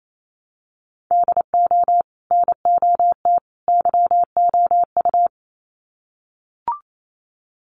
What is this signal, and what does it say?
Morse code, heard as a keyed tone, about 28 wpm, 700 Hz. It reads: DO NOT YOU
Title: Sets of 3 Words 28wpm